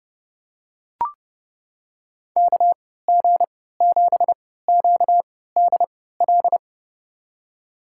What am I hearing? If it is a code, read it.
Morse code, heard as a keyed tone, about 30 wpm, spaced as for 17 wpm, 700 Hz. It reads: KG7QDL
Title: USA Call Signs - 6 Characters - 3x Character Spacing - 30wpm